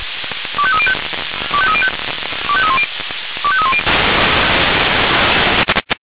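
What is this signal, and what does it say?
Five-tone selective call (ZVEI), heard as quick runs of five tones, tones of about 80 ms. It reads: 25305 25305 25310 25310
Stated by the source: ZVEI five-tone selective call 25220 received on SDR.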